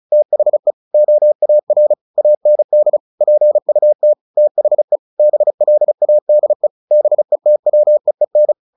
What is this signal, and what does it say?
Morse code, heard as a keyed tone, about 34 wpm, 600 Hz. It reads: THE OAR AND PUT THE BLADE BETWEEN
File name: the old man and the sea 2 35wpm